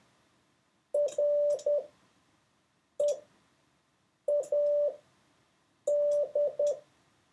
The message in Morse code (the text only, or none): READ